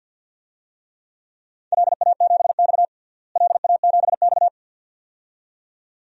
Morse code, heard as a keyed tone, about 51 wpm, 700 Hz. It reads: LA7X LA7X
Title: DXCC-100 Call Signs - 4 Characters - Repeated 2x - 50wpm